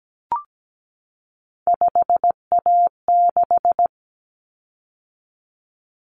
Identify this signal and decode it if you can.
Morse code, heard as a keyed tone, about 17 wpm, 690 Hz. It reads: 5A6
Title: Common Cut Numbers - 3-Digit 17wpm